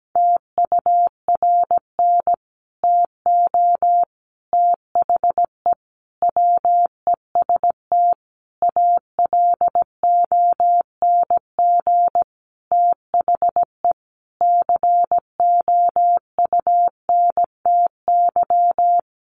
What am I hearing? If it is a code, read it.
Morse code, heard as a keyed tone, about 17 wpm, 700 Hz. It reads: TURN TO THE WEST ALONG THE COUNTY
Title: Sentences from Top 1200 Words 17wpm